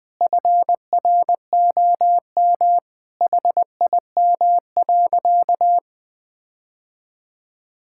Morse code, heard as a keyed tone, about 20 wpm, 700 Hz. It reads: FROM HIM.
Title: Sentences from Top 1600 Words 20wpm